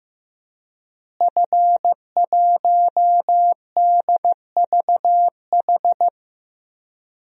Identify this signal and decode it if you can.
Morse code, heard as a keyed tone, about 15 wpm, 700 Hz. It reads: F1DVH
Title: DXCC-100 Call Signs - 5 Characters - Repeated 2x - 15wpm